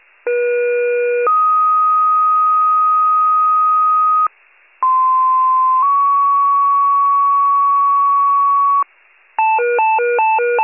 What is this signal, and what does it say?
Two-tone page, heard as a one-second tone, then a three-second tone, about 480 then 1250 Hz, about 1040 then 1120 Hz. It is heard from 0.3 to 8.8 s.